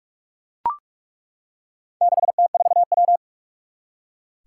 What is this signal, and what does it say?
Morse code, heard as a keyed tone, about 41 wpm, 700 Hz. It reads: 6T4W
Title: Four Challenging Letter-Number Combinations 45wpm